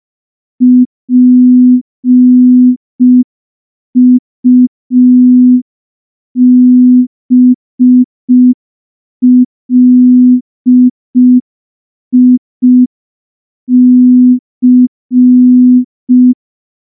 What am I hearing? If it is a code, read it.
Morse code, heard as a keyed tone, about 5 wpm, 250 Hz. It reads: PUBLIC
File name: morse / morse-training-05WPM